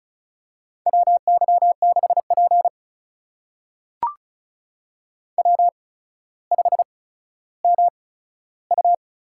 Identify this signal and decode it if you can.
Morse code, heard as a keyed tone, about 36 wpm, spaced as for 15 wpm, 700 Hz. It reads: WY6P W5MU
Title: USA Call Signs - 8x Character Spacing - 35wpm